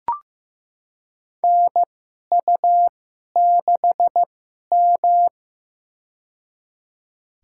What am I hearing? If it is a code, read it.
Morse code, heard as a keyed tone, about 15 wpm, spaced as for 11 wpm, 700 Hz. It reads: NU6M
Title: USA Call Signs - 4 Characters - 2x Character Spacing - 15wpm